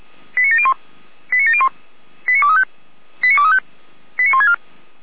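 Five-tone selective call (ZVEI), heard as quick runs of five tones, tones of about 70 ms. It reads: 89701 89701 89235 89235 89164
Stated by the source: ZVEI five-tone selective call 89700 received by SDR.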